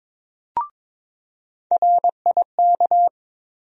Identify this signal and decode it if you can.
Morse code, heard as a keyed tone, about 22 wpm, 700 Hz. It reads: RIK
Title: USA Call Sign Postfixes - 3 Letters 22wpm